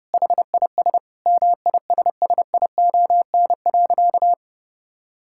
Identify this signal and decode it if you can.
Morse code, heard as a keyed tone, about 32 wpm, 700 Hz. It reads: HIS MISSION.
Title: Sentences from Top 1600 Words 30wpm